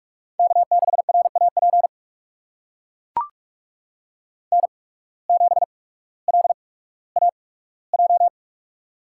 Morse code, heard as a keyed tone, about 44 wpm, spaced as for 19 wpm, 700 Hz. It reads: K6RAP N7LAJ
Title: USA Call Signs - 5 Characters - 8x Character Spacing - 45wpm